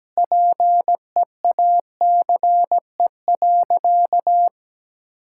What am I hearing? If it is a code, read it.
Morse code, heard as a keyed tone, about 17 wpm, 700 Hz. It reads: PEACE.
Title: Sentences from Top 1600 Words 17wpm